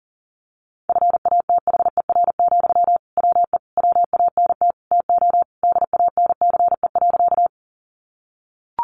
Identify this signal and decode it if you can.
Morse code, heard as a keyed tone, about 40 wpm, 700 Hz. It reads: FATHER, WE WANT TO DANCE.
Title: Sentences from Top 700 Words 40wpm